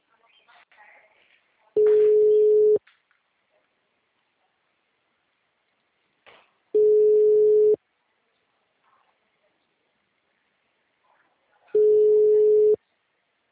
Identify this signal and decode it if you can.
Ringback tone (Europe, 425 Hz).